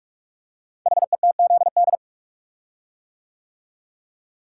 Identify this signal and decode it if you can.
Morse code, heard as a keyed tone, about 44 wpm, 700 Hz. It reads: HETZD
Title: Code Groups - 5 Letters - 45wpm